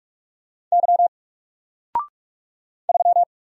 Morse code, heard as a keyed tone, about 45 wpm, 700 Hz. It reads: Y 3
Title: Single Letter-Number 45wpm